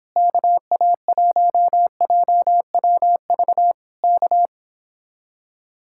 Morse code, heard as a keyed tone, about 26 wpm, 700 Hz. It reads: KA1JWV K
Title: Speed-Racing Common QSO Phrases Encore - 22wpm